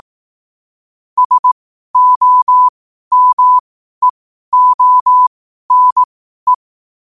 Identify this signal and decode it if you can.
Morse code, heard as a keyed tone, about 18 wpm, spaced as for 12 wpm, 1000 Hz. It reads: SOMEONE
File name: qotdfiles / 12wpm